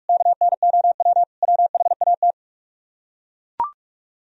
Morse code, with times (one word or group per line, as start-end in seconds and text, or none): 0.1-1.2 KNOW
1.4-2.3 WHAT